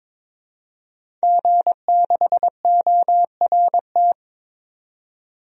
Morse code, heard as a keyed tone, about 22 wpm, 700 Hz. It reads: G6ORT
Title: DXCC-100 Call Signs - 5 Characters - 22wpm